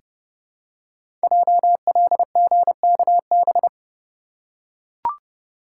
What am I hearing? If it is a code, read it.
Morse code, heard as a keyed tone, about 30 wpm, 690 Hz. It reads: JLGKB
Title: Code Groups - 5 Letters - 30wpm